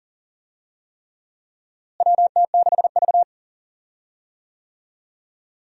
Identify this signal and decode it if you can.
Morse code, heard as a keyed tone, about 40 wpm, 700 Hz. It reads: WT6V
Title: USA Call Signs - 4 Characters - Repeated 3x - 40wpm